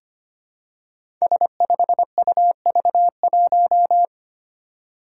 Morse code, heard as a keyed tone, about 25 wpm, 700 Hz. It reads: S5UV1
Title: Five Challenging Letter-Number Combinations 25wpm